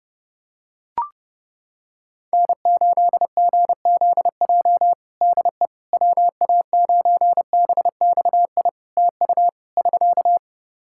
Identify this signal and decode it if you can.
Morse code, heard as a keyed tone, about 32 wpm, 700 Hz. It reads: N8GZJ DE WA9BXI TU <SK>
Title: K1USN SST QSO Practice 30wpm